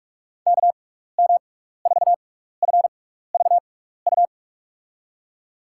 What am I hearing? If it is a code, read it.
Morse code, heard as a keyed tone, about 44 wpm, spaced as for 15 wpm, 700 Hz. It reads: KM4FVU
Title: USA Call Signs - 6x Character Spacing - 45wpm